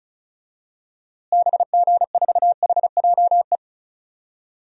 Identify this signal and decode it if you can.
Morse code, heard as a keyed tone, about 34 wpm, 700 Hz. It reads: BG4HJE
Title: DXCC-100 Call Signs - 6 Characters - Repeated 3x - 35wpm